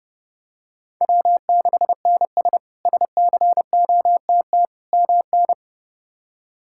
Morse code, heard as a keyed tone, about 30 wpm, 700 Hz. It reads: W6NS SCOTT MN
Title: K1USN SST QSO Practice - 30wpm